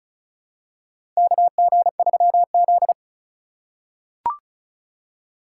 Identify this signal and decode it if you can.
Morse code, heard as a keyed tone, about 34 wpm, 700 Hz. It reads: KG3Z
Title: USA Call Signs - 4 Characters - 4x Character Spacing - 35wpm